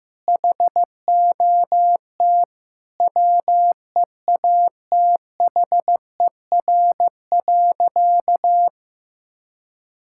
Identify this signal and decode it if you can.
Morse code, heard as a keyed tone, about 15 wpm, 700 Hz. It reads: HOT WEATHER.
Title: Sentences from Top 1000 Words - Review Part 5 - 15wpm